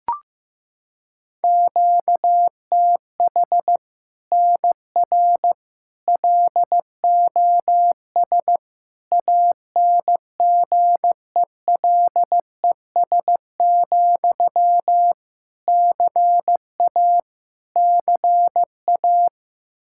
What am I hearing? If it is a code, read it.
Morse code, heard as a keyed tone, about 15 wpm, 700 Hz. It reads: QTH NR LOS ANGELES, CA CA
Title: Practice QSO - Exchange 1 - Beginning - 15wpm